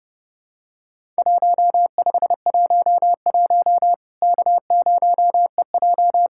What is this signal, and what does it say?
Morse code, heard as a keyed tone, about 30 wpm, 700 Hz. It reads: 1511 K0EJ
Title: Worked All Europe DX Contest - QTC Exchange - 30wpm